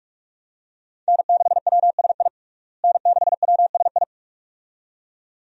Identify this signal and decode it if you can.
Morse code, heard as a keyed tone, about 41 wpm, 700 Hz. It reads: N6WSI N6WSI
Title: USA Call Signs - 5 Characters - Repeated 2x - 45wpm